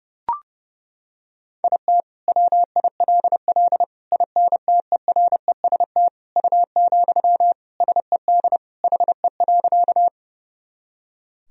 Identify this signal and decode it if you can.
Morse code, heard as a keyed tone, about 32 wpm, 700 Hz. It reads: IT WILL INTEREST U, SED HE.
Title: Sentences from Top 400 Words and Abbreviations 30wpm